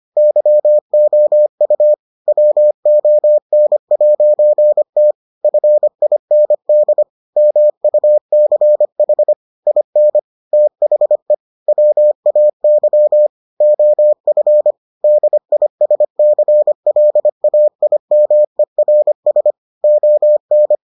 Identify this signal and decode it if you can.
Morse code, heard as a keyed tone, about 25 wpm, 600 Hz. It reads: YOU WON'T FIND MUCH IN THE WAY OF DISCLAIMERS ON